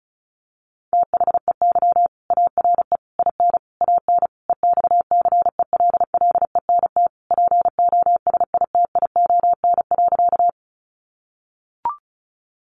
Morse code, heard as a keyed tone, about 34 wpm, 700 Hz. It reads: THEY ARE IN AN EXCELLENT POSITION.
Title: Sentences from Top 1000 Words 35wpm